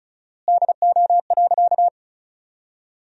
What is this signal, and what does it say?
Morse code, heard as a keyed tone, about 36 wpm, 700 Hz. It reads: DO.
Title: 4x Word-Spacing - Sentences from Top 100 Words - 35wpm